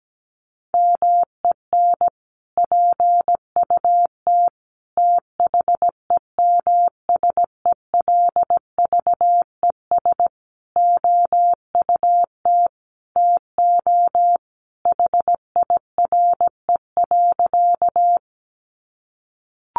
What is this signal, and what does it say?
Morse code, heard as a keyed tone, about 17 wpm, 700 Hz. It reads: MEN PUT THEMSELVES OUT TO HIRE.
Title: Sentences from Top 1400 Words 17wpm